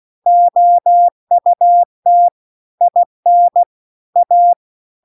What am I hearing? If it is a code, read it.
Morse code, heard as a keyed tone, about 16 wpm, 700 Hz. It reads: OUT IN A